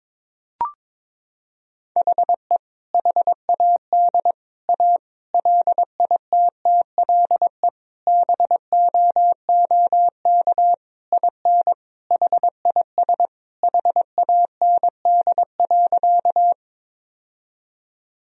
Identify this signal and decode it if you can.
Morse code, heard as a keyed tone, about 22 wpm, 700 Hz. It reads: HE HAD A LITTLE BOOK IN HIS HAND.
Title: Sentences from Top 1000 Words - Review Part 1 - 22wpm